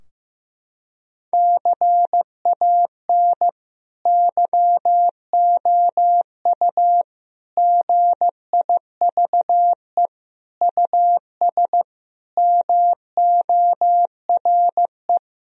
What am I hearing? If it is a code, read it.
Morse code, heard as a keyed tone, about 15 wpm, 700 Hz. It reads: CAN YOU GIVE US MORE
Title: Sentences from Top 1000 Words - Review Part 3 - 15wpm